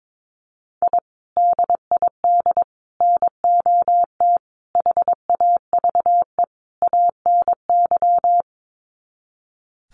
Morse code, heard as a keyed tone, about 22 wpm, 700 Hz. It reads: I DID NOT HAVE ANY
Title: Sets of 5 Words Encore 22wpm